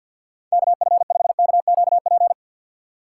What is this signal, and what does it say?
Morse code, heard as a keyed tone, about 51 wpm, 700 Hz. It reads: K F 5 K Q P
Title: USA Call Signs - 6 Characters - 6x Character Spacing - 50wpm